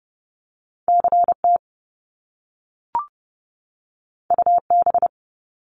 Morse code, heard as a keyed tone, about 30 wpm, 700 Hz. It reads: CT UB
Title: Top Two Letter Combinations in Words 30wpm